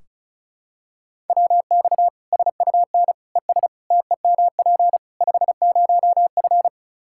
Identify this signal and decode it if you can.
Morse code, heard as a keyed tone, about 34 wpm, 700 Hz. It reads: WX SUN ES TEMP 50F
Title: Practice QSO - Exchange 2 - Info - 35wpm